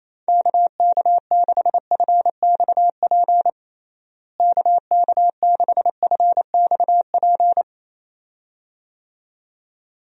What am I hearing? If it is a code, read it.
Morse code, heard as a keyed tone, about 28 wpm, 700 Hz. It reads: KK6FXP KK6FXP
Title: USA Call Signs - 6 Characters - Repeated 2x - 28wpm